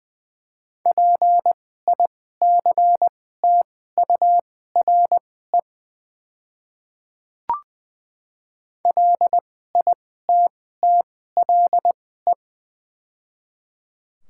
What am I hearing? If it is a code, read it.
Morse code, heard as a keyed tone, about 20 wpm, spaced as for 14 wpm, 700 Hz. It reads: PICTURE LITTLE
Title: 2x Character Spacing - Top 200 Words - 20wpm